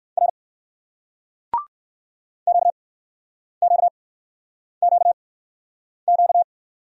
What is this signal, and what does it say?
Morse code, heard as a keyed tone, about 58 wpm, spaced as for 9 wpm, 700 Hz. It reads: R QQQQ